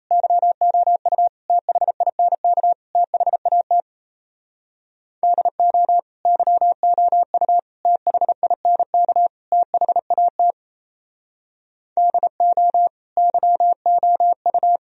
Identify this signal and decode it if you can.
Morse code, heard as a keyed tone, about 35 wpm, 700 Hz. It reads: YOU THINK THAT DO YOU THINK THAT DO YOU